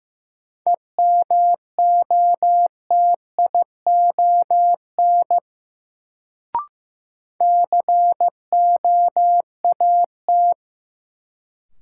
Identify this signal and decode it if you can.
Morse code, heard as a keyed tone, about 15 wpm, 700 Hz. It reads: EMOTION COAT
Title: Sentences from Top 1600 Words - New Word Drill - 15wpm